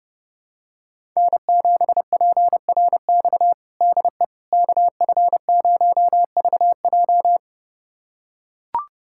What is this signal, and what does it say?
Morse code, heard as a keyed tone, about 30 wpm, 700 Hz. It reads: N7PRX DE KF0VJ
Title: USA Callsign DE Callsign - 5 Characters - 30wpm